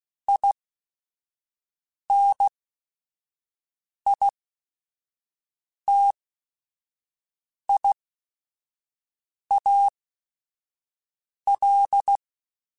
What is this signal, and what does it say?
Morse code, heard as a keyed tone, about 16 wpm, spaced as for 5 wpm, 800 Hz. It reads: INITIAL
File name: Technical Subjects